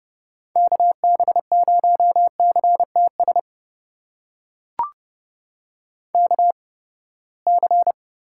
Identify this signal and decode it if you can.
Morse code, heard as a keyed tone, about 30 wpm, spaced as for 21 wpm, 700 Hz. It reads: KB0CTS KC
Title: USA Call Signs - 6 Characters - 8x Character Spacing - 30wpm